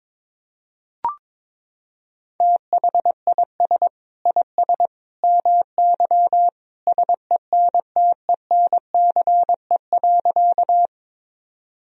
Morse code, heard as a keyed tone, about 22 wpm, 700 Hz. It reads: THIS IS MY SENTENCE.
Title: Sentences from Top 1000 Words - Review Part 1 - 22wpm